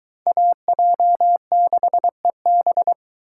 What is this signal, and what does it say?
Morse code, heard as a keyed tone, about 21 wpm, 700 Hz. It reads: AJ6EB